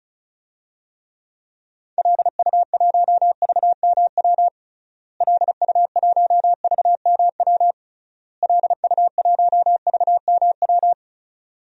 Morse code, heard as a keyed tone, about 34 wpm, 700 Hz. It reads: LU1VMW LU1VMW LU1VMW
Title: DXCC-100 Call Signs - 6 Characters - Repeated 3x - 35wpm